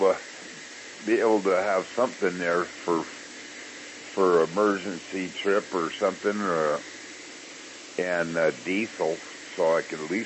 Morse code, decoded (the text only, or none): none